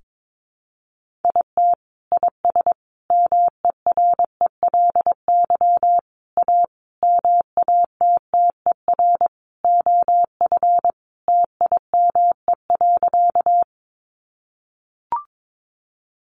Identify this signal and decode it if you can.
Morse code, heard as a keyed tone, about 22 wpm, 700 Hz. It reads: IT IS MERELY A MATTER OF TIME.
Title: Sentences from Top 1400 Words 22wpm